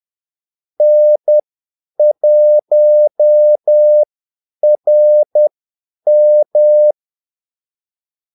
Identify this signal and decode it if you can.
Morse code, heard as a keyed tone, about 10 wpm, spaced as for 8 wpm, 600 Hz. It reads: N1RM
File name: Callsigns-10-8wpm